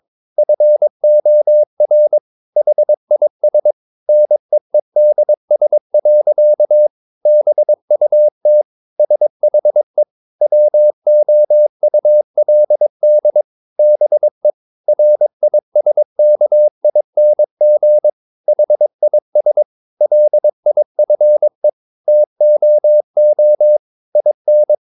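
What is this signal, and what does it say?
Morse code, heard as a keyed tone, about 22 wpm, 600 Hz. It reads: FOR HIS NEEDS. BUT SHE WOULD BE RISKING HIS LIFE TOO IN
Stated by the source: Morse code practice at 22 wpm, sending text from an English novel.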